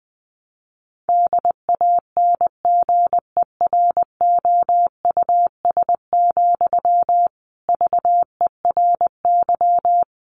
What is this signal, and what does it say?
Morse code, heard as a keyed tone, about 20 wpm, 700 Hz. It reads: DANGEROUS, VERY
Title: Sentences from Top 1200 Words 20wpm